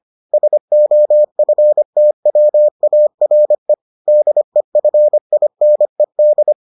Morse code, heard as a keyed tone, about 25 wpm, 600 Hz. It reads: SOFTWARE DEFINED